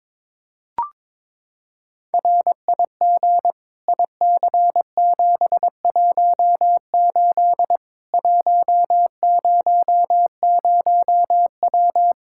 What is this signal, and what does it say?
Morse code, heard as a keyed tone, about 22 wpm, 700 Hz. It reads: RIG IC718 100W
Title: Practice QSO - Exchange 2 - Info - 22wpm